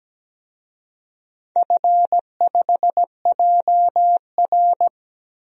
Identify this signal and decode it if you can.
Morse code, heard as a keyed tone, about 17 wpm, 700 Hz. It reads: F5JR
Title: DXCC-100 Call Signs - 4 Characters - Repeated 3x - 17wpm